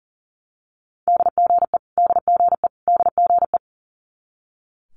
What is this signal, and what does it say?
Morse code, heard as a keyed tone, about 40 wpm, 700 Hz. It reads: DGE DGE DGE
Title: Multi-letter Phonemes Repeated 3x 40wpm